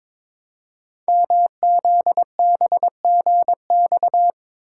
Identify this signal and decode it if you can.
Morse code, heard as a keyed tone, about 22 wpm, 700 Hz. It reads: MZBGX
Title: Code Groups - 5 Letters - 22wpm